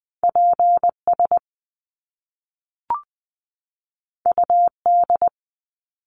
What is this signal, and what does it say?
Morse code, heard as a keyed tone, about 20 wpm, 700 Hz. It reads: P S U D
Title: Top Two Letter Combinations in Words 20wpm